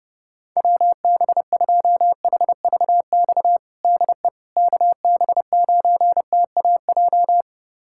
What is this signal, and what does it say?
Morse code, heard as a keyed tone, about 30 wpm, 700 Hz. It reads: WB2HVX DE KB9TAJ
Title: USA Callsign DE Callsign - 6 Characters - 30wpm